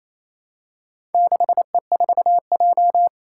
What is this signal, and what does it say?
Morse code, heard as a keyed tone, about 28 wpm, 700 Hz. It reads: 6E4J
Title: Four Challenging Letter-Number Combinations 28wpm